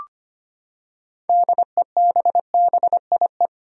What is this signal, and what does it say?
Morse code, heard as a keyed tone, about 25 wpm, 700 Hz. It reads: DEBBIE
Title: USA Common Names - 25wpm